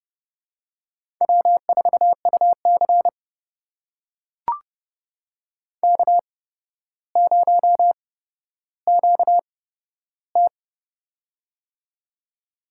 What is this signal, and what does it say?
Morse code, heard as a keyed tone, about 30 wpm, spaced as for 13 wpm, 700 Hz. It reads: W4UC K0QT
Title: USA Call Signs - 4 Characters - 8x Character Spacing - 30wpm